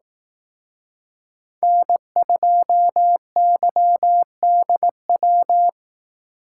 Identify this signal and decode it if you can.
Morse code, heard as a keyed tone, about 18 wpm, 700 Hz. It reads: N2YDW
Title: Speed Racing - USA Call Signs 15wpm